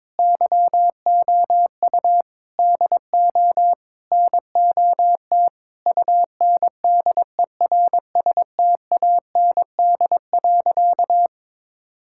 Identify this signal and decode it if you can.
Morse code, heard as a keyed tone, about 22 wpm, 700 Hz. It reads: YOU DO NOT UNDERSTAND.
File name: practice / 22wpm / Sentences with 4-Word Phrases - 22wpm